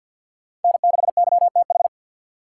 Morse code, heard as a keyed tone, about 47 wpm, 700 Hz. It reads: N6YTH
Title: USA Call Signs - 5 Characters - 3x Character Spacing - 50wpm